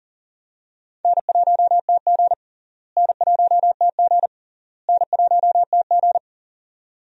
Morse code, heard as a keyed tone, about 37 wpm, 700 Hz. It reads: N1TG N1TG N1TG
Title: USA Call Signs - 4 Characters - Repeated 3x - 40wpm